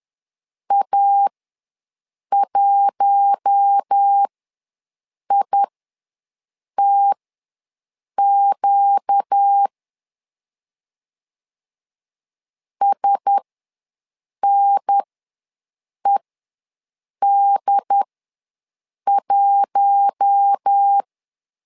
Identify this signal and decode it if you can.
Morse code, heard as a keyed tone, about 10 wpm, spaced as for 6 wpm, 800 Hz. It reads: A1ITQ SNED1
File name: LEZIONI / 4 Q1S ws 10wpm 2208